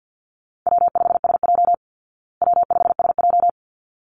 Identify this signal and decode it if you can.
Morse code, heard as a keyed tone, about 49 wpm, 700 Hz. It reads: W5SJ W5SJ
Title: Call Signs Repeated 3x - 50wpm